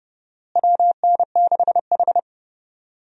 Morse code, heard as a keyed tone, about 29 wpm, 700 Hz. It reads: WN6H
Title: USA Call Signs - 4 Characters - Repeated 2x - 30wpm